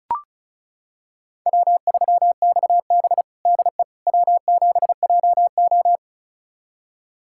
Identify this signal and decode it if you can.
Morse code, heard as a keyed tone, about 34 wpm, 700 Hz. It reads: W3XB DE W7JO
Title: USA Callsign DE Callsign - 4 Characters - 35wpm